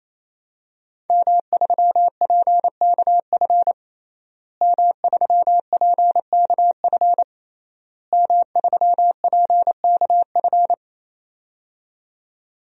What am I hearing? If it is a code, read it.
Morse code, heard as a keyed tone, about 27 wpm, 700 Hz. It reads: M3PKF M3PKF M3PKF
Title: DXCC-100 Call Signs - 5 Characters - Repeated 3x - 28wpm